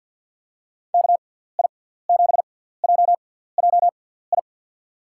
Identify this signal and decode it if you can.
Morse code, heard as a keyed tone, about 48 wpm, spaced as for 17 wpm, 700 Hz. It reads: KI7JJI
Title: USA Call Signs - 6x Character Spacing - 50wpm